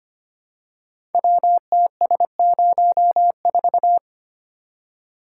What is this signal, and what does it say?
Morse code, heard as a keyed tone, about 25 wpm, 690 Hz. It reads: WTS04
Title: Code Groups - 5 Letter:Numbers - 25wpm